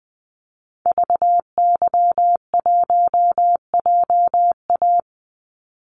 Morse code, heard as a keyed tone, about 20 wpm, 700 Hz. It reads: VY1JA
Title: Call Signs Repeated 3x - 20wpm